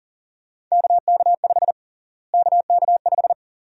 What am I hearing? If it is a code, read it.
Morse code, heard as a keyed tone, about 40 wpm, 700 Hz. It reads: KK5 KK5
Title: USA Call Sign Prefixes - Repeated 2x - 40wpm